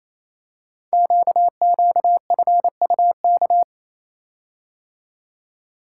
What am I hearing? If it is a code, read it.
Morse code, heard as a keyed tone, about 28 wpm, 700 Hz. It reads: QQFUK
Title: Code Groups - 5 Letters - 28wpm